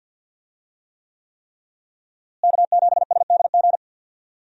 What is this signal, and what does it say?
Morse code, heard as a keyed tone, about 52 wpm, 700 Hz. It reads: K7SDG
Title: USA Call Signs - 5 Characters - Repeated 3x - 50wpm